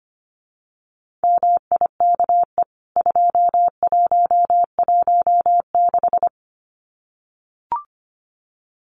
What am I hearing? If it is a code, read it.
Morse code, heard as a keyed tone, about 25 wpm, 700 Hz. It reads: MIKE 2116
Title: CWT Exchange 25wpm